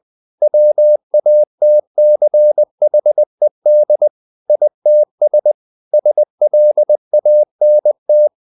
Morse code, heard as a keyed tone, about 20 wpm, 600 Hz. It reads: WATCHED ITS SLANT
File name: the old man and the sea / the old man and the sea part 1 20wpm